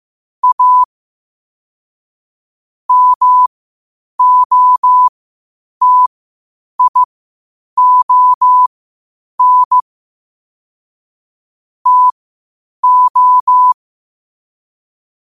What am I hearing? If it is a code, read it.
Morse code, heard as a keyed tone, about 15 wpm, spaced as for 8 wpm, 1000 Hz. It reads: A MOTION TO
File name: qotdfiles / 8wpm